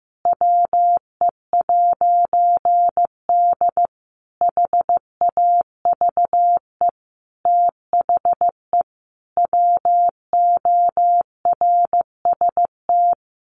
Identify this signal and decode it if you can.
Morse code, heard as a keyed tone, about 15 wpm, 700 Hz. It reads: WE'D HAVE THE WORST
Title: A Sound Of Thunder 15wpm